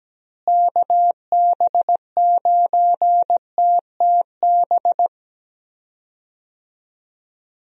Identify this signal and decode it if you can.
Morse code, heard as a keyed tone, about 17 wpm, 700 Hz. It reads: KB9TTB
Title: USA Call Signs - 6 Characters - Repeated 2x - 17wpm